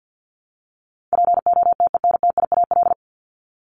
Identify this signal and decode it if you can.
Morse code, heard as a keyed tone, about 50 wpm, 700 Hz. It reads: POTENTIAL
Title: Top 501-750 Words 50wpm